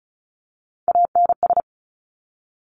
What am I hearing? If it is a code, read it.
Morse code, heard as a keyed tone, about 36 wpm, 700 Hz. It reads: ANS
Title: Top Three Letter Combinations in Words 35wpm